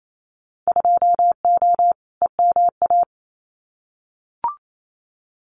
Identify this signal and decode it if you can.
Morse code, heard as a keyed tone, about 28 wpm, 700 Hz. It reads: 2O EMA
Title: Winter Field Day Exchanges 28wpm